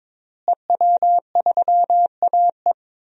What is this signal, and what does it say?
Morse code, heard as a keyed tone, about 22 wpm, 700 Hz. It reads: EW3AE
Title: DXCC-100 Call Signs - 5 Characters - Repeated 2x - 22wpm